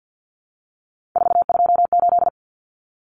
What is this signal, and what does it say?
Morse code, heard as a keyed tone, about 50 wpm, 700 Hz. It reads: 428
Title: Common Cut Numbers - 3-Digit 50wpm